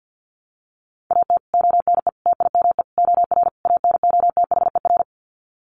Morse code, heard as a keyed tone, about 50 wpm, 700 Hz. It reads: AT ONE TIME OR ANOTHER
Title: Sets of 5 Words Encore 50wpm